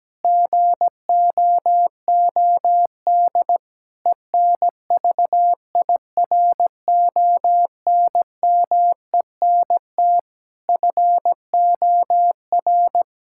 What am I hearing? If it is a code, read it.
Morse code, heard as a keyed tone, about 17 wpm, 700 Hz. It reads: GOOD ENVIRONMENT FOR
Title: Sentences from Top 1000 Words - Review Part 5 - 17wpm